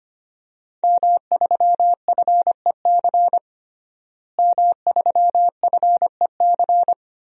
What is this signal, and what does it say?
Morse code, heard as a keyed tone, about 25 wpm, 700 Hz. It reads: M3FEC M3FEC
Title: DXCC-100 Call Signs - 5 Characters - Repeated 2x - 25wpm